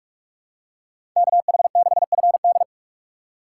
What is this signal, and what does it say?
Morse code, heard as a keyed tone, about 46 wpm, 700 Hz. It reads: KH6FD
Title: USA Call Signs - 5 Characters - 3x Character Spacing - 45wpm